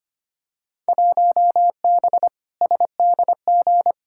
Morse code, heard as a keyed tone, about 25 wpm, 700 Hz. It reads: 1B SDG
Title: Field Day Exchanges - 25wpm